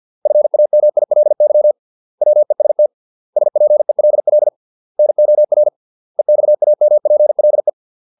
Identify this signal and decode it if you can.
Morse code, heard as a keyed tone, about 50 wpm, 600 Hz. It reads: FAMILY WEST SPELL NOR EXAMPLE